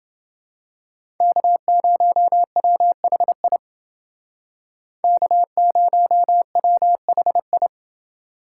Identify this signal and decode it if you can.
Morse code, heard as a keyed tone, about 29 wpm, 700 Hz. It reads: K0WHI K0WHI